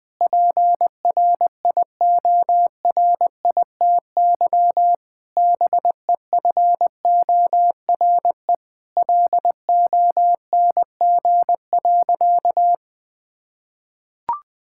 Morse code, heard as a keyed tone, about 20 wpm, 700 Hz. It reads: PRIORITY BEFORE LONG.
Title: Sentences from Top 1600 Words 20wpm